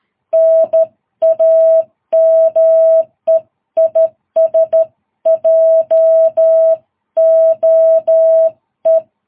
Morse code, heard as a keyed tone, about 12 wpm, 660 Hz. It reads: NAME IS JOE
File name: jh8cbh / 2022cw